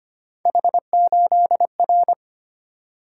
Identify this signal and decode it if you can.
Morse code, heard as a keyed tone, about 25 wpm, 700 Hz. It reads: H8R
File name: Three Letter-Number Combinations - 25wpm